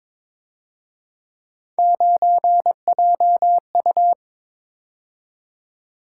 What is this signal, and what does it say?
Morse code, heard as a keyed tone, about 26 wpm, 700 Hz. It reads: 9JU
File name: Speed-Racing - Three Letter-Number Combinations - 22wpm